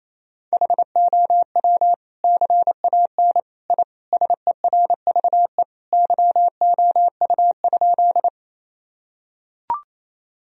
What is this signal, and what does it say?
Morse code, heard as a keyed tone, about 28 wpm, 700 Hz. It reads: HOW CAN I SERVE YOU?
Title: Sentences from Top 1000 Words - Review Part 2 - 28wpm